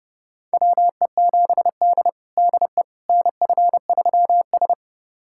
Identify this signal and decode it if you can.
Morse code, heard as a keyed tone, about 30 wpm, 700 Hz. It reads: WE7D DE NF3S
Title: USA Callsign DE Callsign - 4 Characters - 30wpm